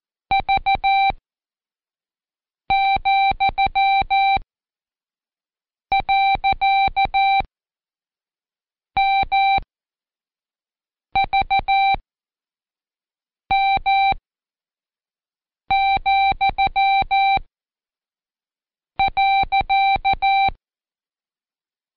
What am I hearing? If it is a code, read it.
Morse code, heard as a keyed tone, about 14 wpm, spaced as for 5 wpm, 780 Hz. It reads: V,.MVM,.
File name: Code Quick / AudioDownld1 / Seg65-Day7-Pract Comma